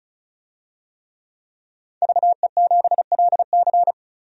Morse code, heard as a keyed tone, about 36 wpm, 700 Hz. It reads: VE7LC
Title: DXCC-100 Call Signs - 5 Characters - 35wpm